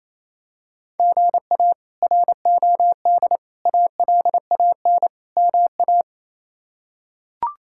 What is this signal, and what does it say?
Morse code, heard as a keyed tone, about 28 wpm, 700 Hz. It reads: GA ROD ALAN MA
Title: K1USN SST QSO Practice - 28wpm